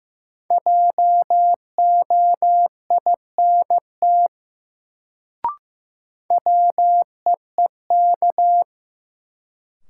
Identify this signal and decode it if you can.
Morse code, heard as a keyed tone, about 15 wpm, 700 Hz. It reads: JOINT WEEK
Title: Sets of 5 Words Encore - New Word Drill - 15wpm